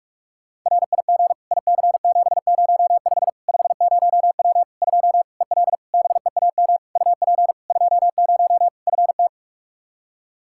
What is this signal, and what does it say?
Morse code, heard as a keyed tone, about 42 wpm, 700 Hz. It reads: RIG IC705 50W 2 EL BEAM UP 20 FT
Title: Practice QSO - Exchange 2 - Info - 45wpm